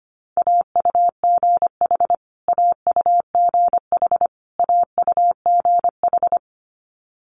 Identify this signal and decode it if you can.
Morse code, heard as a keyed tone, about 25 wpm, 700 Hz. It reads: AUGH AUGH AUGH
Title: Multi-letter Phonemes Repeated 3x 25wpm